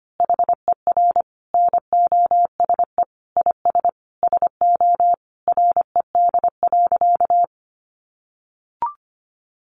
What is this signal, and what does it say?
Morse code, heard as a keyed tone, about 25 wpm, 700 Hz. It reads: HER NOSE IS SO RED.
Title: Easy Phrases 25wpm